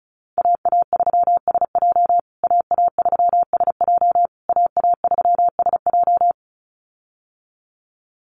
Morse code, heard as a keyed tone, about 34 wpm, 700 Hz. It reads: AA3SJ AA3SJ AA3SJ
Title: USA Call Signs Repeated 3x 35wpm